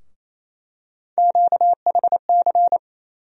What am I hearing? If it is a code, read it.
Morse code, heard as a keyed tone, about 28 wpm, 700 Hz. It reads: QHC
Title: USA Call Sign Postfixes - 3 Letters 28wpm